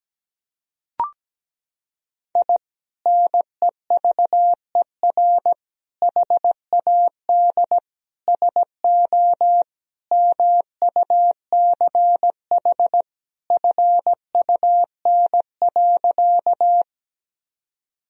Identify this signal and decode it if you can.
Morse code, heard as a keyed tone, about 17 wpm, 700 Hz. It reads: I NEVER HAD SO MUCH FUN.
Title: Sentences from Top 1000 Words - Review Part 4 - 17wpm